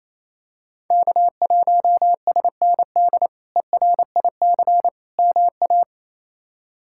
Morse code, heard as a keyed tone, about 28 wpm, 700 Hz. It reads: K1SND ERIC MA
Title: K1USN SST QSO Practice - 28wpm